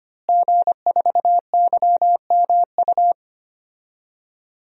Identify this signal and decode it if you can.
Morse code, heard as a keyed tone, about 25 wpm, 700 Hz. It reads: G4YMU
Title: DXCC-100 Call Signs - 5 Characters - Repeated 2x - 25wpm